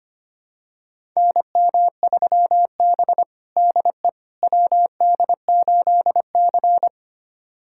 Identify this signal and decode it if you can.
Morse code, heard as a keyed tone, about 25 wpm, 700 Hz. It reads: NM3B DE WD8C
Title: USA Callsign DE Callsign - 4 Characters - 25wpm